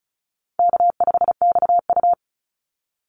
Morse code, heard as a keyed tone, about 34 wpm, 700 Hz. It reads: K5XU
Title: Call Signs - 35wpm